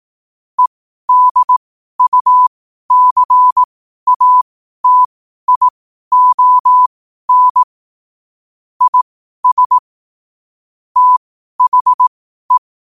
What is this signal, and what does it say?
Morse code, heard as a keyed tone, about 18 wpm, spaced as for 12 wpm, 1000 Hz. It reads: EDUCATION IS THE
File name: qotdfiles / 12wpm